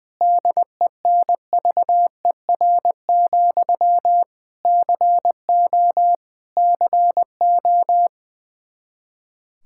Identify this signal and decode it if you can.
Morse code, heard as a keyed tone, about 20 wpm, 700 Hz. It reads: DENVER, CO CO
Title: Practice QSO - Exchange 1 - Beginning - 20wpm